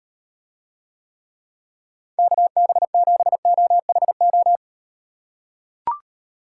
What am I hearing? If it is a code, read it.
Morse code, heard as a keyed tone, about 38 wpm, 700 Hz. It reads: KB7OHO